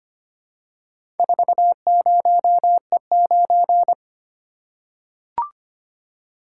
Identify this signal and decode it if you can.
Morse code, heard as a keyed tone, about 25 wpm, 700 Hz. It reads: 40E9
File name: practice / 25wpm / Four Letter-Number Combinations - 25wpm